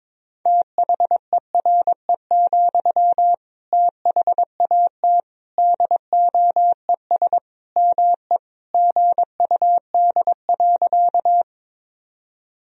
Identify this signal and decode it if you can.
Morse code, heard as a keyed tone, about 22 wpm, 700 Hz. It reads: THERE, THAT DOES ME GUD.
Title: Sentences from Top 200 Words and Abbreviations 22wpm